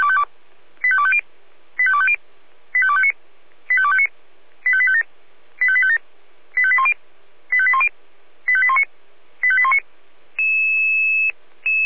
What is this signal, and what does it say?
Five-tone selective call (ZVEI), heard as quick runs of five tones, tones of about 70 ms. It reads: 85250 85250 85259 85259 85857 85857 85810 85810 85819 85819
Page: SDR/ZVEI